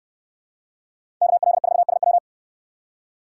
Morse code, heard as a keyed tone, about 63 wpm, 700 Hz. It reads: KL4SW